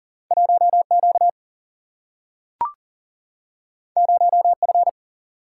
Morse code, heard as a keyed tone, about 40 wpm, 700 Hz. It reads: 1Q 0F
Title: Single-Letter Single-Number 40wpm